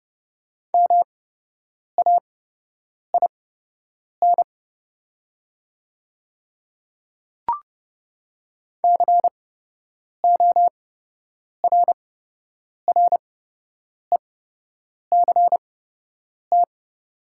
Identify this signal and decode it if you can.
Morse code, heard as a keyed tone, about 30 wpm, spaced as for 8 wpm, 700 Hz. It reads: MAIN CORRECT
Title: 8x Character Spacing - Top 500 Words - 30wpm